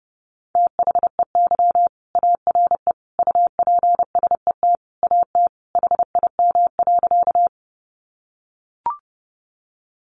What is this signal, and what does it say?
Morse code, heard as a keyed tone, about 30 wpm, 700 Hz. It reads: THEY ARE UPSET AT HIM.
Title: Sentences from Top 800 Words 30wpm